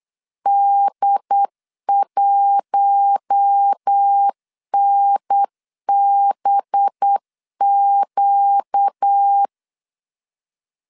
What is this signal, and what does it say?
Morse code, heard as a keyed tone, about 8 wpm, 800 Hz. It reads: D1NBQ
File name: CW / DET / 1 6 1500C S B 8wpm 5142